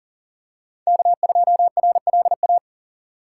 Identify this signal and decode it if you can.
Morse code, heard as a keyed tone, about 41 wpm, 690 Hz. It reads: K2RLA
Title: Code Groups - 5 Letter:Numbers - 40wpm